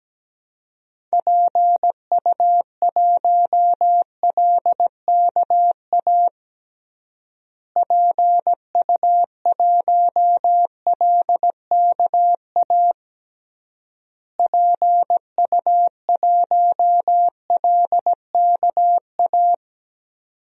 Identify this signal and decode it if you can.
Morse code, heard as a keyed tone, about 17 wpm, 700 Hz. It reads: PU1LKA PU1LKA PU1LKA
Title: DXCC-100 Call Signs - 6 Characters - Repeated 3x - 17wpm